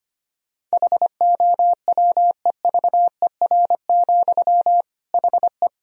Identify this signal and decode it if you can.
Morse code, heard as a keyed tone, about 25 wpm, 700 Hz. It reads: HOWEVER, HE
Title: Sentences from Top 1000 Words - Review Part 3 - 25wpm